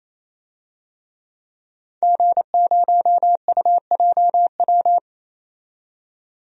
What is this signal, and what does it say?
Morse code, heard as a keyed tone, about 28 wpm, 700 Hz. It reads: G0UJW
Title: DXCC-100 Call Signs - 5 Characters - 28wpm